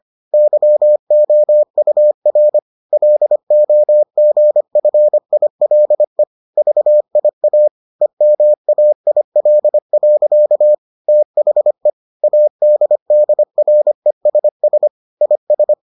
Morse code, heard as a keyed tone, about 25 wpm, 600 Hz. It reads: YOUR LOGFILE VIA EMAIL. THE ADDRESS IS